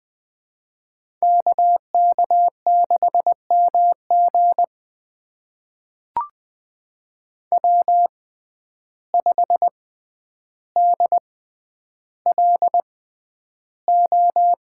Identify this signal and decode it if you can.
Morse code, heard as a keyed tone, about 20 wpm, spaced as for 10 wpm, 700 Hz. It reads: KK6MG W5DLO
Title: USA Call Signs - 5 Characters - 6x Character Spacing - 20wpm